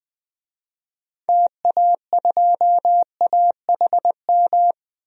Morse code, heard as a keyed tone, about 20 wpm, 700 Hz. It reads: TA2AHM
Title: DXCC-100 Call Signs - 6 Characters - 20wpm